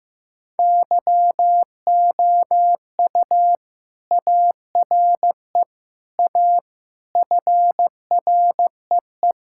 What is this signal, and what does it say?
Morse code, heard as a keyed tone, about 15 wpm, 700 Hz. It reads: YOU ARE A FREE
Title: Sentences from Top 1600 Words 15wpm